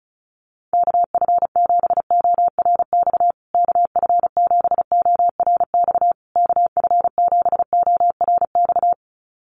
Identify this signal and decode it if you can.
Morse code, heard as a keyed tone, about 34 wpm, 700 Hz. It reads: KF7ORX KF7ORX KF7ORX
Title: USA Call Signs Repeated 3x 35wpm